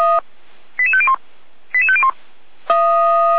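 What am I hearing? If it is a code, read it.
Five-tone selective call (ZVEI), heard as quick runs of five tones, tones of about 70 ms. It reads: 88501 88501
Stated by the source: ZVEI five-tone selective call received by SDR.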